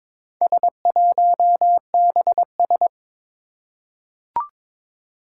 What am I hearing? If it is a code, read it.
Morse code, heard as a keyed tone, about 22 wpm, 700 Hz. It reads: S1BS
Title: Four Challenging Letter-Number Combinations 22wpm